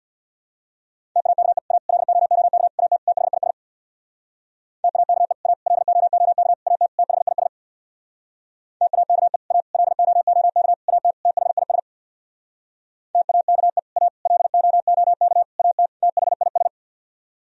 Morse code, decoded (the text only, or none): TAKE A LOOK AT THIS TAKE A LOOK AT THIS TAKE A LOOK AT THIS TAKE A LOOK AT THIS